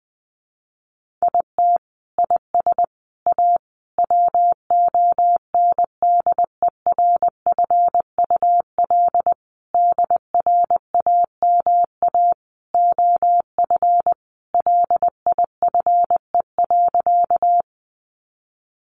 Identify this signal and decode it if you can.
Morse code, heard as a keyed tone, about 19 wpm, 700 Hz. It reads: IT IS A WONDERFUL DRAMA OF LIFE.
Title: Sentences from Top 1500 Words 20wpm